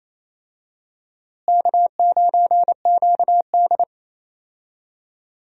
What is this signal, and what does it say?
Morse code, heard as a keyed tone, about 28 wpm, 700 Hz. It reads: K9QD
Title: USA Call Signs - 4 Characters - Repeated 2x - 28wpm